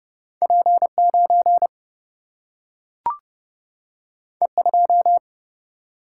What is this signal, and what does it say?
Morse code, heard as a keyed tone, about 30 wpm, 700 Hz. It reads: P9 E2
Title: Single-Letter Single-Number 30wpm